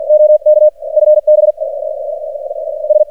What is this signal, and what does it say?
Morse code, heard as a keyed tone, about 44 wpm, 610 Hz, with radio noise. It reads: OK FB S
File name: archived / CRSnet